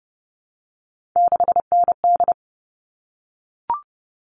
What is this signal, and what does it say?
Morse code, heard as a keyed tone, about 30 wpm, 700 Hz. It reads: 6ND
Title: All Cut Numbers - 3-Digit 30wpm